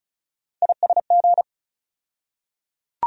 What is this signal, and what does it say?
Morse code, heard as a keyed tone, about 34 wpm, 700 Hz. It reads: ISG
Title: USA Call Sign Postfixes - 3 Letters 35wpm